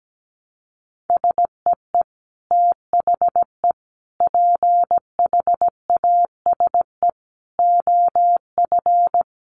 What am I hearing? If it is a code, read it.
Morse code, heard as a keyed tone, about 17 wpm, 700 Hz. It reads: SEE THE PHASE OF
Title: Sentences from Top 1200 Words 17wpm